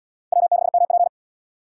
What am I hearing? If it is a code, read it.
Morse code, heard as a keyed tone, about 68 wpm, 700 Hz. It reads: W6MP